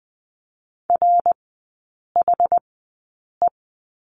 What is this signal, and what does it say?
Morse code, heard as a keyed tone, about 20 wpm, spaced as for 8 wpm, 700 Hz. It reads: RHE
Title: 060 Lesson 11e - Letter h - Three Characters 20wpm